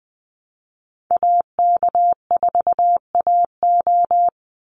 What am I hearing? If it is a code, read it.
Morse code, heard as a keyed tone, about 20 wpm, 700 Hz. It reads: AK4AO
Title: Call Signs Repeated 3x - 20wpm